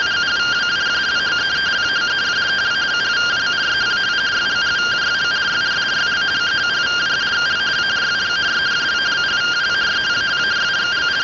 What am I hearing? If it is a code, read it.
Two-tone teleprinter signal, about 170 Hz shift.